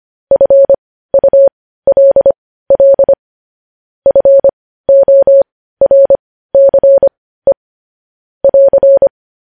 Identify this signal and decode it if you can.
Morse code, heard as a keyed tone, about 25 wpm, spaced as for 15 wpm, 550 Hz. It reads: FULL FORCE <AR>
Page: news headlines in morse code at 15 wpm